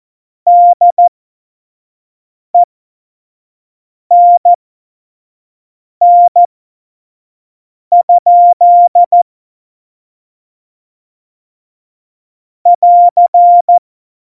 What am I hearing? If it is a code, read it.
Morse code, heard as a keyed tone, about 14 wpm, spaced as for 5 wpm, 700 Hz. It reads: DENN? <AR>
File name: Morsekurs / Witze / Witz 13. Februar 2018, 25 F70